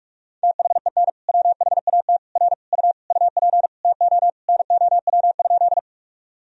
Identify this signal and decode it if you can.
Morse code, heard as a keyed tone, about 44 wpm, 700 Hz. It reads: THEN WHAT R U UP TO NOW?
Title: Sentences from Top 100 Words and Abbreviations 45wpm